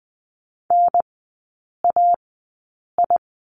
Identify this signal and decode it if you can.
Morse code, heard as a keyed tone, about 20 wpm, spaced as for 8 wpm, 700 Hz. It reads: NAI
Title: 035 Lesson 7e - Letter s - Three Characters 20wpm